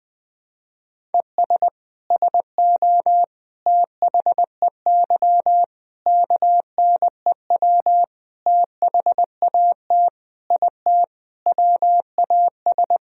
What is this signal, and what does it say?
Morse code, heard as a keyed tone, about 21 wpm, 700 Hz. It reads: ES SO THEY KNEW THAT IT WAS
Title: Sentences from Top 400 Words and Abbreviations 20wpm